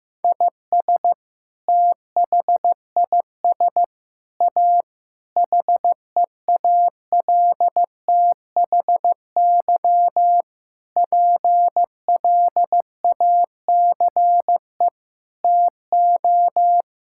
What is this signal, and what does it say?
Morse code, heard as a keyed tone, about 15 wpm, 700 Hz. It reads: IS THIS A HEALTHY PLACE TO
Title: Sentences from Top 1000 Words - Review Part 5 - 15wpm